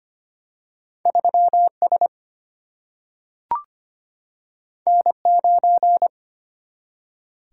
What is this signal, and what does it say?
Morse code, heard as a keyed tone, about 25 wpm, 700 Hz. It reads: 3S N9
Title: Single-Letter Single-Number 25wpm